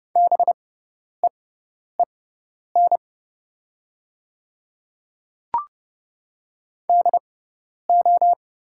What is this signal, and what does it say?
Morse code, heard as a keyed tone, about 29 wpm, spaced as for 10 wpm, 700 Hz. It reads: BEEN DO
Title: 6x Character Spacing - Top 100 Words - 30wpm